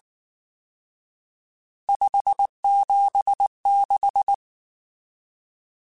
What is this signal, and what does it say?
Morse code, heard as a keyed tone, about 19 wpm, 800 Hz. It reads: 576
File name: RSTs / RST-mp3s / Various RST @18